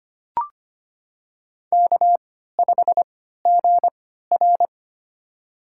Morse code, heard as a keyed tone, about 25 wpm, spaced as for 14 wpm, 700 Hz. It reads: K5GR
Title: USA Call Signs - 4 Characters - 3x Character Spacing - 25wpm